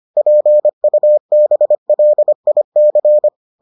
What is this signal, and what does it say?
Morse code, heard as a keyed tone, about 25 wpm, 600 Hz. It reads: PUBLIC